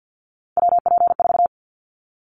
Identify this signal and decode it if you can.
Morse code, heard as a keyed tone, about 50 wpm, 700 Hz. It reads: WP4
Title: Callsign Prefixes for Top 25 DXCC - 50wpm